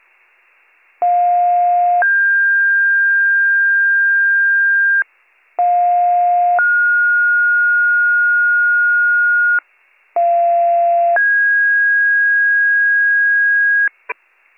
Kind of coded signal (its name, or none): two-tone page